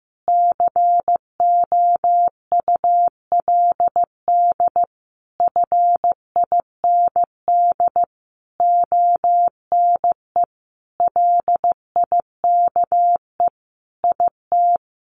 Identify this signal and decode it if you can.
Morse code, heard as a keyed tone, about 15 wpm, 700 Hz. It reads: COULD FIND ONE LIKE IT
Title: Sentences from Top 1400 Words 15wpm